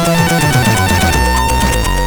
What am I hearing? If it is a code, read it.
There is no code.